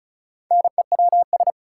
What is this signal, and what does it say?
Morse code, heard as a keyed tone, about 34 wpm, 700 Hz. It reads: NEWS
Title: Four Letter Words - 35wpm